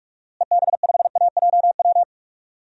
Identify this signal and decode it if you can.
Morse code, heard as a keyed tone, about 44 wpm, 700 Hz. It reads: EB5AJW
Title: DXCC-100 Call Signs - 6 Characters - Repeated 3x - 45wpm